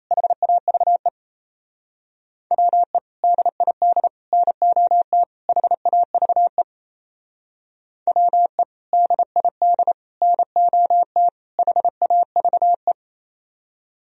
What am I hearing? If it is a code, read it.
Morse code, heard as a keyed tone, about 36 wpm, 700 Hz. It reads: HAVE WE DID NOT HAVE WE DID NOT HAVE